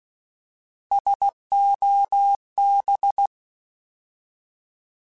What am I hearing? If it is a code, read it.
Morse code, heard as a keyed tone, about 16 wpm, 800 Hz. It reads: SOB